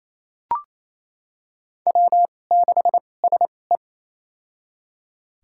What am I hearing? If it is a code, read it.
Morse code, heard as a keyed tone, about 27 wpm, spaced as for 20 wpm, 700 Hz. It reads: W6SE
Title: USA Call Signs - 2x Character Spacing - 28wpm